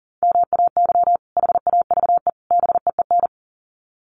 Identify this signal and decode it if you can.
Morse code, heard as a keyed tone, about 40 wpm, 700 Hz. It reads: MAY HAVE BEEN
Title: Sets of 3 Words 40wpm